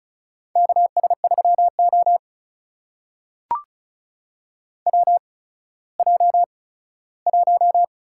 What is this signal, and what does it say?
Morse code, heard as a keyed tone, about 36 wpm, spaced as for 17 wpm, 700 Hz. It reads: KS3O WJ1
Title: USA Call Signs - 4 Characters - 8x Character Spacing - 35wpm